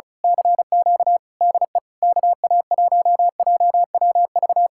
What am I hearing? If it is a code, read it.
Morse code, heard as a keyed tone, about 32 wpm, 700 Hz. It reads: CQ DE KA1JWV
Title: Speed-Racing Common QSO Phrases Encore - 35wpm